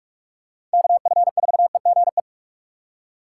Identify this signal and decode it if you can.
Morse code, heard as a keyed tone, about 46 wpm, 700 Hz. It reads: KF4EGE
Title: USA Call Signs - 6 Characters - 2x Character Spacing - 45wpm